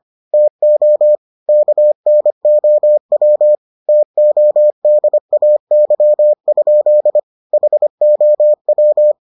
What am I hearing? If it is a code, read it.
Morse code, heard as a keyed tone, about 25 wpm, 600 Hz. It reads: TO KNOW TODAY? HOW